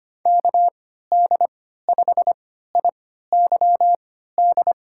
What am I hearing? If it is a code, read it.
Morse code, heard as a keyed tone, about 25 wpm, spaced as for 14 wpm, 700 Hz. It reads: KD5IYD
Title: USA Call Signs - 6 Characters - 3x Character Spacing - 25wpm